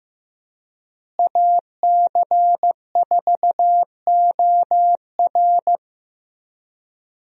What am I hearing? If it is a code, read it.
Morse code, heard as a keyed tone, about 15 wpm, 700 Hz. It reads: AC4OR
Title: USA Call Signs - 5 Characters - Repeated 3x - 15wpm